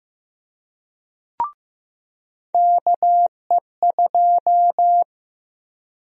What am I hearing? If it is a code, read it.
Morse code, heard as a keyed tone, about 15 wpm, 700 Hz. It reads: KE2
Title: USA Call Sign Prefixes - Repeated 2x - 15wpm